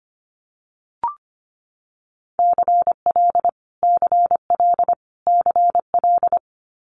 Morse code, heard as a keyed tone, about 25 wpm, 700 Hz. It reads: CL CL CL
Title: Multi-letter Phonemes Repeated 3x 25wpm